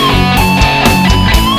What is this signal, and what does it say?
There is no code.